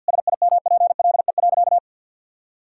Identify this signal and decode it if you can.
Morse code, heard as a keyed tone, about 52 wpm, 700 Hz. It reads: SIMPLE.
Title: Sentences from Top 1000 Words - Review Part 6 - 50wpm